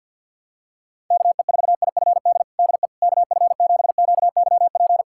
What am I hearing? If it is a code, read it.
Morse code, heard as a keyed tone, about 46 wpm, 700 Hz. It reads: KE4IFD DE KF7QYP
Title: USA Callsign DE Callsign - 6 Characters - 50wpm